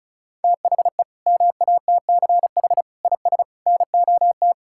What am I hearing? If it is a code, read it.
Morse code, heard as a keyed tone, about 36 wpm, 700 Hz. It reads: THE MATCH IS NOT
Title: Sentences from Top 1600 Words 35wpm